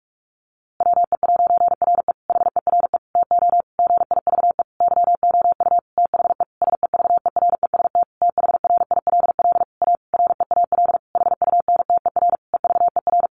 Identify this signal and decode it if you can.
Morse code, heard as a keyed tone, about 44 wpm, 700 Hz. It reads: WE'RE HERE TO GIVE YOU THE SEVEREST THRILL A REAL HUNTER EVER
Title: A Sound Of Thunder 45wpm